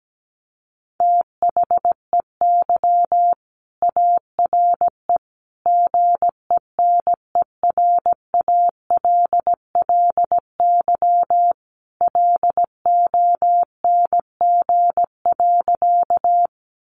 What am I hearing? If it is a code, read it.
Morse code, heard as a keyed tone, about 17 wpm, 700 Hz. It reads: THEY ARE GENERALLY LONG.
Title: Sentences from Top 600 Words 17wpm